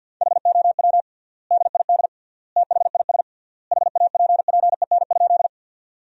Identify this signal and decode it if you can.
Morse code, heard as a keyed tone, about 50 wpm, 700 Hz. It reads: HOW DID THIS HAPPEN?
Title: 3x Word-Spacing - Sentences from Top 400 Words - 50wpm